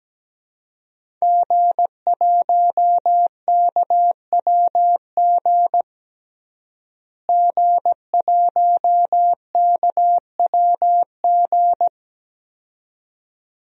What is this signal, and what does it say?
Morse code, heard as a keyed tone, about 17 wpm, 700 Hz. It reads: G1KWG G1KWG
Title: DXCC-100 Call Signs - 5 Characters - Repeated 2x - 17wpm